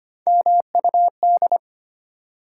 Morse code, heard as a keyed tone, about 25 wpm, 700 Hz. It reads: MUD
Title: Three Letter Words - 25wpm